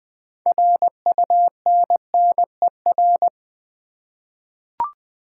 Morse code, heard as a keyed tone, about 20 wpm, 700 Hz. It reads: RUNNER
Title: 6 Letter Words - First 3 Letters Given - 20wpm